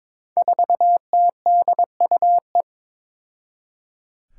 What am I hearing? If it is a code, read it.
Morse code, heard as a keyed tone, about 22 wpm, 700 Hz. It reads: 4TDUE
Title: Five Challenging Letter-Number Combinations 22wpm